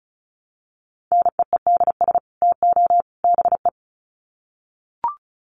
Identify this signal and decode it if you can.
Morse code, heard as a keyed tone, about 34 wpm, 700 Hz. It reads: NEEDS TO BE
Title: Sets of 3 Words 35wpm